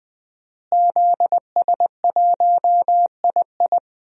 Morse code, heard as a keyed tone, about 21 wpm, 700 Hz. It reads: ZS1II
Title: DXCC-100 Call Signs - 5 Characters - Repeated 3x - 20wpm